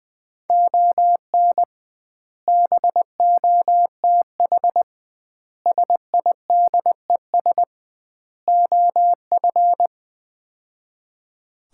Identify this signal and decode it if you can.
Morse code, heard as a keyed tone, about 21 wpm, 700 Hz. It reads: ON BOTH SIDES OF
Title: 2x Word-Spacing - Sets of 4 Words - 20wpm